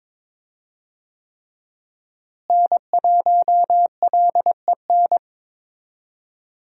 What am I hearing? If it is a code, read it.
Morse code, heard as a keyed tone, about 22 wpm, 700 Hz. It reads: N1LEN
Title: USA Call Signs - 5 Characters - Repeated 3x - 22wpm